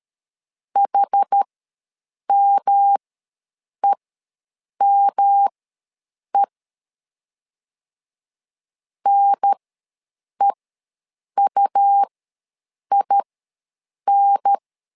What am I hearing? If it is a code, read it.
Morse code, heard as a keyed tone, about 12 wpm, spaced as for 7 wpm, 800 Hz. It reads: HMEME NEUIN
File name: LEZIONI / 3 MWRO / 3 M ws 12wpm 1745